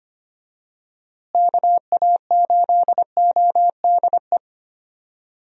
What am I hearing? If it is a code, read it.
Morse code, heard as a keyed tone, about 25 wpm, 700 Hz. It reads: KA8ODE
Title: USA Call Signs - 6 Characters - Repeated 2x - 25wpm